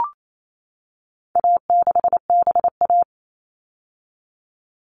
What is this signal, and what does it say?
Morse code, heard as a keyed tone, about 28 wpm, 700 Hz. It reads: A6BA
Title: All Cut Numbers - 4-Digit 28wpm